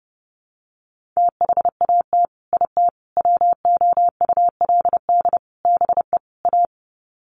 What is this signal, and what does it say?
Morse code, heard as a keyed tone, about 30 wpm, 700 Hz. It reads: THAT IT WOULD BE A
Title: Sets of 5 Words Encore2 30wpm